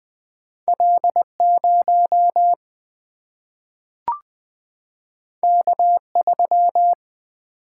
Morse code, heard as a keyed tone, about 20 wpm, 700 Hz. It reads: L 0 K 3
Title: Single-Letter Single-Number 20wpm